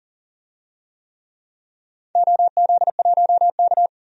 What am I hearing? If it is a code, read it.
Morse code, heard as a keyed tone, about 37 wpm, 700 Hz. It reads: OZ1K
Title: DXCC-100 Call Signs - 4 Characters - 40wpm